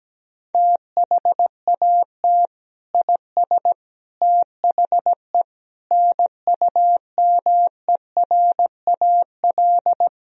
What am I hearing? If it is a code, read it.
Morse code, heard as a keyed tone, about 17 wpm, 700 Hz. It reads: THAT IS THE NUMERAL
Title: Sentences from Top 1000 Words - Review Part 2 - 17wpm